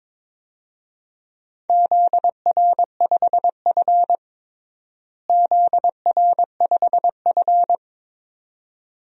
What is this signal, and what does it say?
Morse code, heard as a keyed tone, about 22 wpm, 700 Hz. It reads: ZR5F ZR5F
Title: DXCC-100 Call Signs - 4 Characters - Repeated 2x - 22wpm